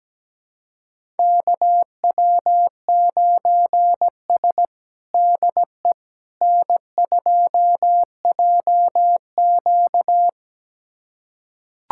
Morse code, heard as a keyed tone, about 17 wpm, 700 Hz. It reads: KW9S DE N2JQ
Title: USA Callsign DE Callsign - 4 Characters - 17wpm